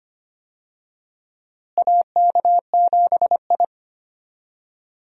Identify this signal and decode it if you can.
Morse code, heard as a keyed tone, about 25 wpm, 700 Hz. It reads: AK7I
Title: USA Call Signs - 4 Characters - Repeated 2x - 25wpm